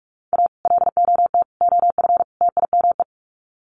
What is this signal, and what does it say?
Morse code, heard as a keyed tone, about 44 wpm, 700 Hz. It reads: A LOT OF TIME
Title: Sets of 4 Words 45wpm